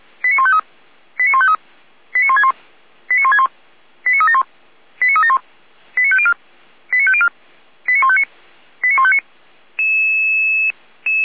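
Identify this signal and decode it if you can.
Five-tone selective call (ZVEI), heard as quick runs of five tones, tones of about 70 ms. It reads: 89153 89153 89161 89161 89371 89371 89504 89504 89159 89159
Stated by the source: ZVEI five-tone selective call received with an SDR.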